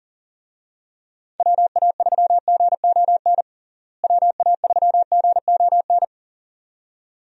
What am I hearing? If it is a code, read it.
Morse code, heard as a keyed tone, about 40 wpm, 700 Hz. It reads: WA3GON WA3GON
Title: USA Call Signs - 6 Characters - Repeated 2x - 40wpm